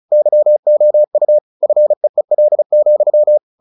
Morse code, heard as a keyed tone, about 34 wpm, 600 Hz. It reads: YOU FEEL,
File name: the old man and the sea / the old man and the sea 1 35wpm